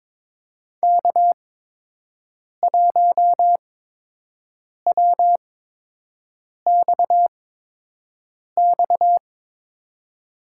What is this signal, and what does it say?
Morse code, heard as a keyed tone, about 23 wpm, spaced as for 6 wpm, 700 Hz. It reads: K1WXX
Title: USA Call Signs - 5 Characters - 8x Character Spacing - 22wpm